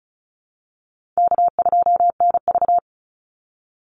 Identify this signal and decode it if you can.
Morse code, heard as a keyed tone, about 34 wpm, 700 Hz. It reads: K2NV
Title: Call Signs Repeated 2x - 35wpm